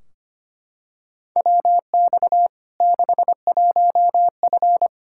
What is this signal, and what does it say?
Morse code, heard as a keyed tone, about 25 wpm, 700 Hz. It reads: WX 61F
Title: K1USN SST QSO Practice 25wpm